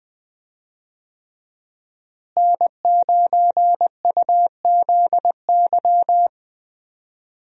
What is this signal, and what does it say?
Morse code, heard as a keyed tone, about 20 wpm, 700 Hz. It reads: N9UZY
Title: USA Call Signs - 5 Characters - Repeated 3x - 20wpm